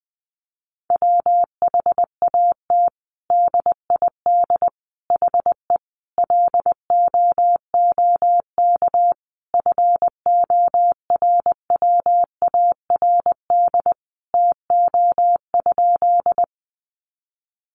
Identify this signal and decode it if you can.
Morse code, heard as a keyed tone, about 20 wpm, 700 Hz. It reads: WHAT DID HE LOOK FORWARD TO?
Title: Sentences from Top 1300 Words 20wpm